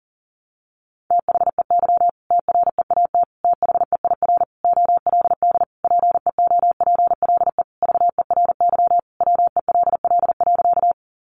Morse code, heard as a keyed tone, about 40 wpm, 700 Hz. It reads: THEY TREAT THEIR OLD PEOPLE VERY WELL.
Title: Sentences from Top 1300 Words 40wpm